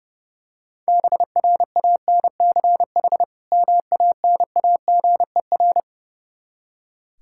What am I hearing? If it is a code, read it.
Morse code, heard as a keyed tone, about 30 wpm, 700 Hz. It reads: BRANCH MANAGER
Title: K1USN SST QSO Practice 30wpm